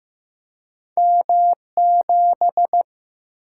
Morse code, heard as a keyed tone, about 15 wpm, 700 Hz. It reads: M7
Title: Single-Letter Single-Number 15wpm